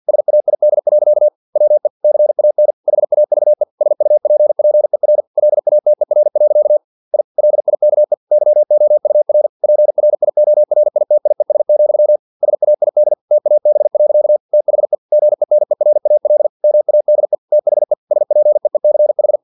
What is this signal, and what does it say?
Morse code, heard as a keyed tone, about 50 wpm, 600 Hz. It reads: SAID. WE CAN HAVE SUPPER LATER. I LIKE YOUR PRIORITIES, SAID TAB. THE GENERAL MADE THE SPEECH
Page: Never 49wpm